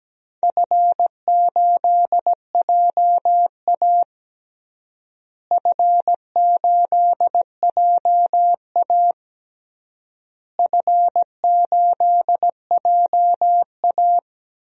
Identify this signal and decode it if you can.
Morse code, heard as a keyed tone, about 17 wpm, 700 Hz. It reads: F8JA F8JA F8JA
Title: DXCC-100 Call Signs - 4 Characters - Repeated 3x - 17wpm